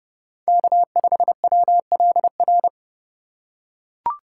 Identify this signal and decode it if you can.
Morse code, heard as a keyed tone, about 30 wpm, 700 Hz. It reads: K5WLR
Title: USA Call Signs - 5 Characters - 3x Character Spacing - 30wpm